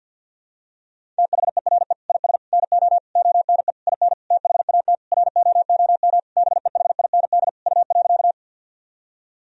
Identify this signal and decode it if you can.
Morse code, heard as a keyed tone, about 49 wpm, 700 Hz. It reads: THERE IS NO ONE IN THAT ROOM BEHIND U.
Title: Sentences from Top 500 Words and Abbreviations 50wpm